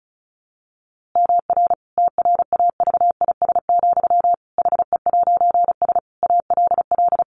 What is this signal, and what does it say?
Morse code, heard as a keyed tone, about 34 wpm, 700 Hz. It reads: MR TRAVIS, HE'S ALL
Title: A Sound Of Thunder 35wpm